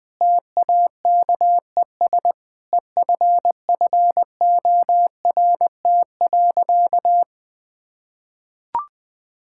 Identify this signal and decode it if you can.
Morse code, heard as a keyed tone, about 20 wpm, 700 Hz. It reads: TAKES EFFORT.
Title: Sentences with 3-Word Phrases - 20wpm